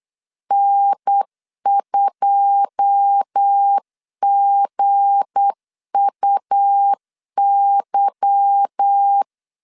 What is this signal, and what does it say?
Morse code, heard as a keyed tone, about 8 wpm, 800 Hz. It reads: N2GUY
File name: LEZIONI / 9 2 8wpm 3437